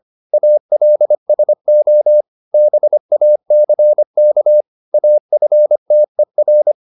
Morse code, heard as a keyed tone, about 25 wpm, 600 Hz. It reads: ALSO BACK AFTER
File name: Top-100-Words-25